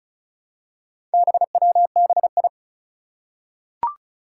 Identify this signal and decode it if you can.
Morse code, heard as a keyed tone, about 36 wpm, 700 Hz. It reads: BWBI
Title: Four Challenging Letter-Number Combinations 35wpm